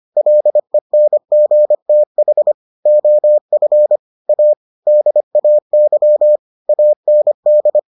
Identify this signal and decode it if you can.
Morse code, heard as a keyed tone, about 25 wpm, 600 Hz. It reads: LENGTH OF A DAY AND